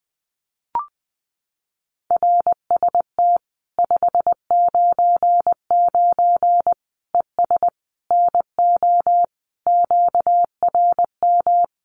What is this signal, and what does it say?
Morse code, heard as a keyed tone, about 20 wpm, 700 Hz. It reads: RST 599 ES NO QRM
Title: Common QSO Phrases 20wpm